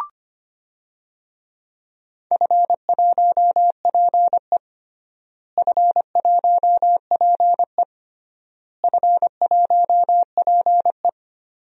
Morse code, heard as a keyed tone, about 25 wpm, 700 Hz. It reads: F1PE F1PE F1PE
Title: DXCC-100 Call Signs - 4 Characters - Repeated 3x - 25wpm